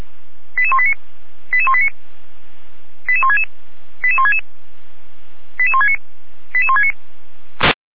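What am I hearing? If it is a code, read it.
Five-tone selective call (ZVEI), heard as quick runs of five tones, tones of about 80 ms. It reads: 88189 88189 88166 88166 88169 88169
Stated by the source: ZVEI five-tone selective call received on SDR.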